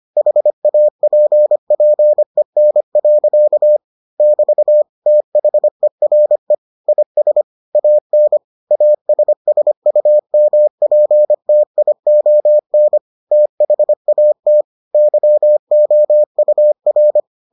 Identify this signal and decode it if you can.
Morse code, heard as a keyed tone, about 25 wpm, 600 Hz. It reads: HAPPEN. <BT> THERE IS AN ASSUMPTION THAT YOUR